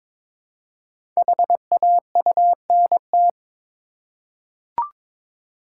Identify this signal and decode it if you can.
Morse code, heard as a keyed tone, about 22 wpm, 700 Hz. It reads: HAUNT
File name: 5 Letter Words - First 3 Letters Given - 22wpm